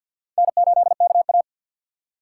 Morse code, heard as a keyed tone, about 50 wpm, 700 Hz. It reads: N8KA
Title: USA Call Signs - 4 Characters - 6x Character Spacing - 50wpm